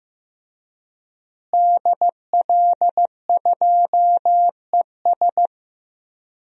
Morse code, heard as a keyed tone, about 16 wpm, 700 Hz. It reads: DL2ES
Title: DXCC-100 Call Signs - 5 Characters - Repeated 2x - 15wpm